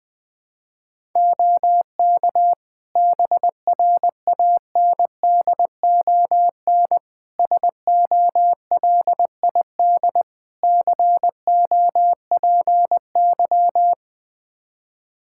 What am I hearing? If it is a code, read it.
Morse code, heard as a keyed tone, about 20 wpm, 700 Hz. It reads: OK BRANDON SOLID COPY
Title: K1USN SST QSO Practice 20wpm